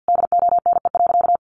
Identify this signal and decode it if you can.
Morse code, heard as a keyed tone, about 50 wpm, 700 Hz. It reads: DONE.
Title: Sentences from Top 700 Words 50wpm